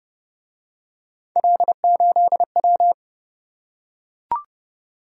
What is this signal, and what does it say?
Morse code, heard as a keyed tone, about 27 wpm, 700 Hz. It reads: L8W